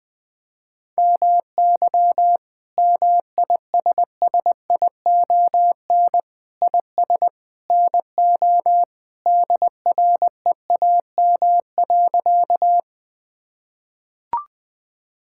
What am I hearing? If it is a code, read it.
Morse code, heard as a keyed tone, about 20 wpm, 700 Hz. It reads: MY MISSION IS NO DREAM.
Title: Sentences from Top 1600 Words 20wpm